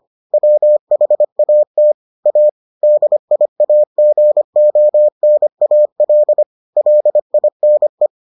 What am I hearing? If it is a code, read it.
Morse code, heard as a keyed tone, about 25 wpm, 600 Hz. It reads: WHAT A DIAGONAL LINE